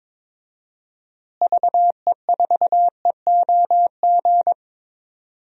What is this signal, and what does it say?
Morse code, heard as a keyed tone, about 22 wpm, 700 Hz. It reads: VE4EOG
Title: DXCC-100 Call Signs - 6 Characters - Repeated 3x - 22wpm